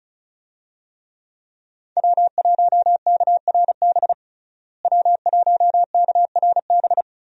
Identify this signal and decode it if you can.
Morse code, heard as a keyed tone, about 34 wpm, 700 Hz. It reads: W1KRB W1KRB
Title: USA Call Signs - 5 Characters - Repeated 2x - 35wpm